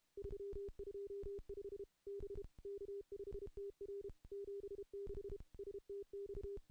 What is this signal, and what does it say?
Morse code, heard as a keyed tone, about 32 wpm, 400 Hz. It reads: 325 B K5TR 76 STX
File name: contests / CW-SS-k5tr- -0000